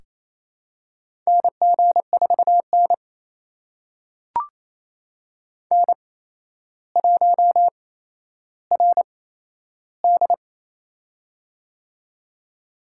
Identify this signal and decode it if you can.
Morse code, heard as a keyed tone, about 28 wpm, spaced as for 12 wpm, 700 Hz. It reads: NG4N N1RD
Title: USA Call Signs - 4 Characters - 8x Character Spacing - 28wpm